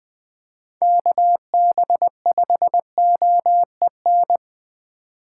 Morse code, heard as a keyed tone, about 20 wpm, 700 Hz. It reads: KB5OEN
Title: USA Call Signs - 2x Character Spacing - 20wpm